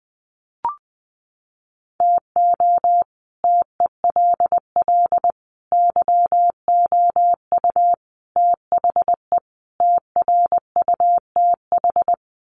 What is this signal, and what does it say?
Morse code, heard as a keyed tone, about 19 wpm, 700 Hz. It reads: TO TELL YOU THE TRUTH
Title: Sets of 5 Words 20wpm